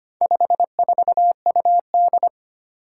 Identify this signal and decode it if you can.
Morse code, heard as a keyed tone, about 25 wpm, 700 Hz. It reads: 54UD
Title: Four Challenging Letter-Number Combinations 25wpm